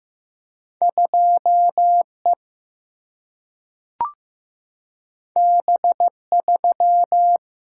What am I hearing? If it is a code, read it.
Morse code, heard as a keyed tone, about 15 wpm, 700 Hz. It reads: 2E B3
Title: Single-Letter Single-Number 15wpm